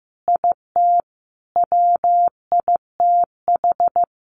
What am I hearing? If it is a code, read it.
Morse code, heard as a keyed tone, about 15 wpm, 700 Hz. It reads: IT WITH
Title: Sentences from Top 1300 Words 15wpm